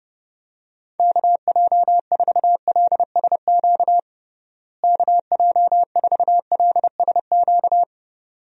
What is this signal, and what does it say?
Morse code, heard as a keyed tone, about 30 wpm, 700 Hz. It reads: KJ4LSQ KJ4LSQ
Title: USA Call Signs - 6 Characters - Repeated 2x - 30wpm